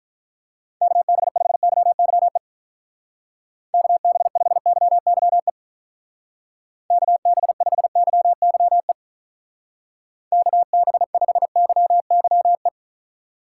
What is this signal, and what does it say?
Morse code, heard as a keyed tone, about 43 wpm, 700 Hz. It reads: KB5YYE KB5YYE KB5YYE KB5YYE
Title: Speed Racing - USA Call Signs 35wpm